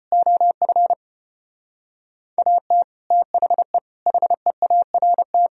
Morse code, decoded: OF AT THE HEART

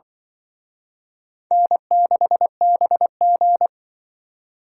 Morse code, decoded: N6BG

